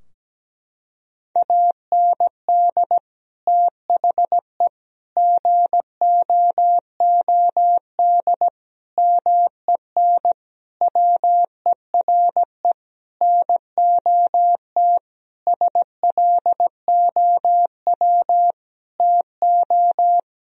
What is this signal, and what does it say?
Morse code, heard as a keyed tone, about 17 wpm, 700 Hz. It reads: AND THE GOOD MEN WERE NOT SLOW TO
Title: Sentences from Top 1000 Words - Review Part 2 - 17wpm